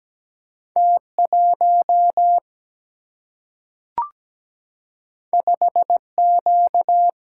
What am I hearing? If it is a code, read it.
Morse code, heard as a keyed tone, about 17 wpm, 700 Hz. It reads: T1 5Q
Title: Single-Letter Single-Number 17wpm